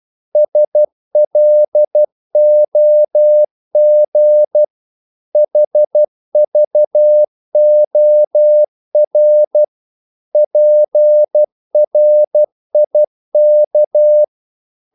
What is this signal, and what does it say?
Morse code, heard as a keyed tone, about 12 wpm, 600 Hz. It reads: SLOG HVOR PRIK
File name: Korte ord DK 12wpm